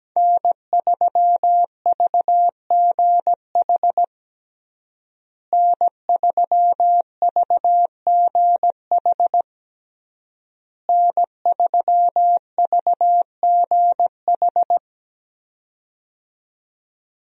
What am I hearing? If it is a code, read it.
Morse code, heard as a keyed tone, about 17 wpm, 700 Hz. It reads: N3VGH N3VGH N3VGH
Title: USA Call Signs - 5 Characters - Repeated 3x - 17wpm